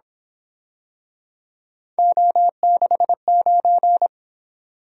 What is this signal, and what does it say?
Morse code, heard as a keyed tone, about 27 wpm, 700 Hz. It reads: O69
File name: Speed-Racing - Three Letter-Number Combinations - 22wpm